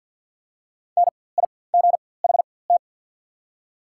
Morse code, heard as a keyed tone, about 48 wpm, spaced as for 23 wpm, 700 Hz. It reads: NIGHT
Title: 4x Character Spacing - Top 300 Words - 50wpm